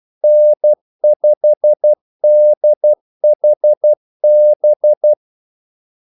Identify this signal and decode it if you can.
Morse code, heard as a keyed tone, about 12 wpm, 600 Hz. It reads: N5DHB
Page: 2 ekstra ordmellemrum mellem kaldesignalerne